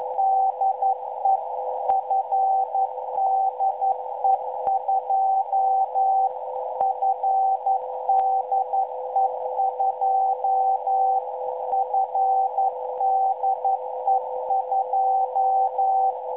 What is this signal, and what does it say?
Morse code, heard as a keyed tone, about 11 wpm, 770 Hz, with noise. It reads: DE FDE2 FDE2 FDE2
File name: BEACONS / FDE2-7.039